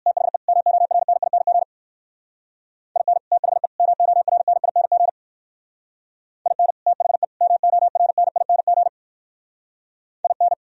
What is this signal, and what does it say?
Morse code, heard as a keyed tone, about 61 wpm, 700 Hz. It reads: THE MORNING IN THE MORNING IN THE MORNING IN